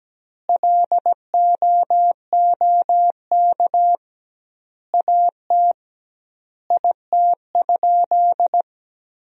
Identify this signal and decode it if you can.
Morse code, heard as a keyed tone, about 17 wpm, 700 Hz. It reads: LOOK AT IT?
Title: 2x Word-Spacing - Sentences from Top 100 Words - 17wpm